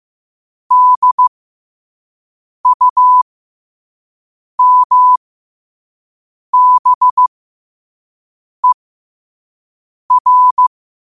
Morse code, heard as a keyed tone, about 15 wpm, spaced as for 5 wpm, 1000 Hz. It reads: DUMBER
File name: qotdfiles / 5wpm